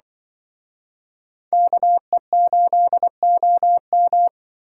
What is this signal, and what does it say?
Morse code, heard as a keyed tone, about 23 wpm, 700 Hz. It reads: KE8OM
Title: Speed Racing - USA 2x2 Call Signs 20wpm